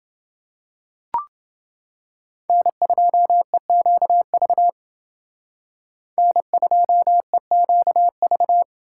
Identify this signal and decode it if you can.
Morse code, heard as a keyed tone, about 28 wpm, 700 Hz. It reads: N2EQV N2EQV